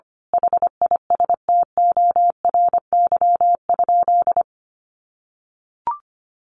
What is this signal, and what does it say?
Morse code, heard as a keyed tone, about 25 wpm, 700 Hz. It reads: HISTORY?
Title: Sentences from Top 1100 Words 25wpm